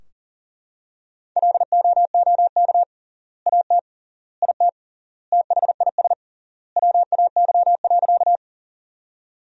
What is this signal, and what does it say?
Morse code, heard as a keyed tone, about 40 wpm, 700 Hz. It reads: LOOK AT IT THIS WAY.
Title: 3x Word-Spacing - Sentences from Top 100 Words - 40wpm